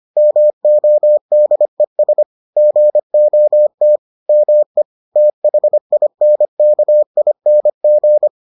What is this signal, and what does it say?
Morse code, heard as a keyed tone, about 25 wpm, 600 Hz. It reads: MODES GOT ME THINKING